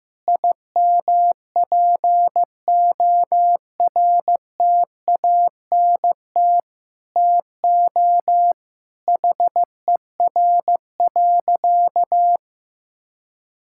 Morse code, heard as a keyed tone, about 15 wpm, 700 Hz. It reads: IMPORTANT TO HER.
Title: Sentences from Top 1000 Words - Review Part 5 - 15wpm